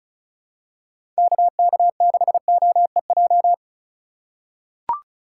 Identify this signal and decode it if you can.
Morse code, heard as a keyed tone, about 36 wpm, 700 Hz. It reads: KK6OEJ
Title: USA Call Signs - 6 Characters - 3x Character Spacing - 35wpm